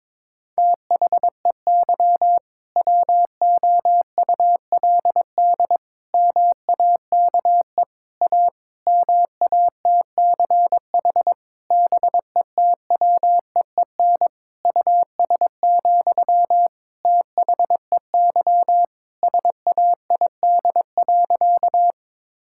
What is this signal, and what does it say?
Morse code, heard as a keyed tone, about 22 wpm, 700 Hz. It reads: THEY WOULD MAKE A MATCH BETWEEN US, THEY SAID.
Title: Sentences from Top 1600 Words 22wpm